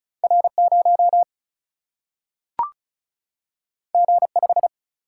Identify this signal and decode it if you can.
Morse code, heard as a keyed tone, about 36 wpm, 700 Hz. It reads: R 0 G 5
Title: Single-Letter Single-Number 35wpm